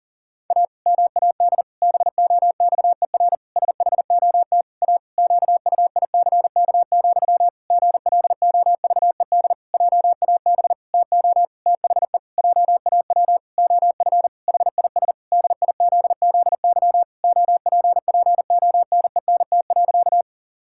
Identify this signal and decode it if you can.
Morse code, heard as a keyed tone, about 40 wpm, 700 Hz. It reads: A MAD BOXER SHOT A QUICK, GLOVED JAB TO THE JAW OF HIS DIZZY OPPONENT.
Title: Pangram Challenge v2 - 40wpm